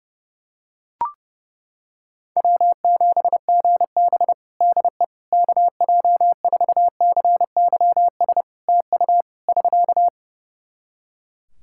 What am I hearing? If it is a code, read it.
Morse code, heard as a keyed tone, about 30 wpm, 700 Hz. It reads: W7GB DE KJ4CYS TU <SK>
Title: Practice QSO - Exchange 4 - Ending - 30wpm